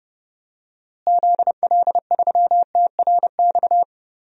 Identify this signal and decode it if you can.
Morse code, heard as a keyed tone, about 30 wpm, 700 Hz. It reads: ZL3TRX
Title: DXCC-100 Call Signs - 6 Characters - Repeated 3x - 30wpm